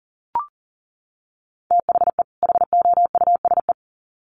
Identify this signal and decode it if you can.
Morse code, heard as a keyed tone, about 40 wpm, 700 Hz. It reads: THE HOUSE
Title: Sets of 2 Words 40wpm